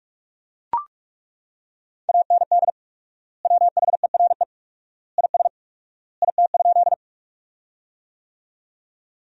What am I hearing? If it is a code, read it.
Morse code, heard as a keyed tone, about 41 wpm, 700 Hz. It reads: AND WHERE IS IT?
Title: 4x Word-Spacing - Sentences from Top 200 Words - 45wpm